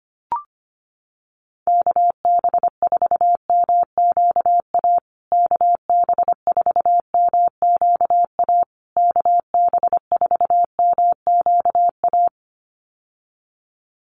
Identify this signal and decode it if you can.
Morse code, heard as a keyed tone, about 25 wpm, 700 Hz. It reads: KB4MQA KB4MQA KB4MQA
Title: USA Call Signs Repeated 3x 25wpm